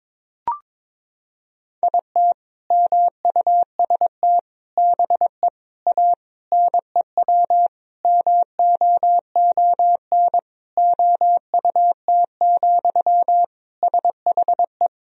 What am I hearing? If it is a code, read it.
Morse code, heard as a keyed tone, about 22 wpm, 700 Hz. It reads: IT MUST BE A NEW MOON OUT, SHE
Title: Sentences from Top 500 Words and Abbreviations 22wpm